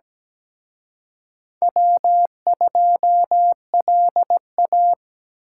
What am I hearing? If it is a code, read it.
Morse code, heard as a keyed tone, about 21 wpm, 700 Hz. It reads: W2LA
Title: Speed Racing - USA 1x2 Call Signs 17wpm